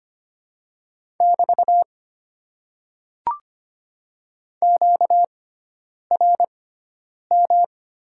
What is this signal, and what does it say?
Morse code, heard as a keyed tone, about 26 wpm, spaced as for 9 wpm, 690 Hz. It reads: <BT> QRM
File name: Top 90 QSO Elements - 6x Character Spacing - 25wpm